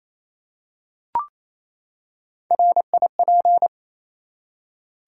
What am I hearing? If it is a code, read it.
Morse code, heard as a keyed tone, about 27 wpm, 700 Hz. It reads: RIP
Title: Three Letter Words - 28wpm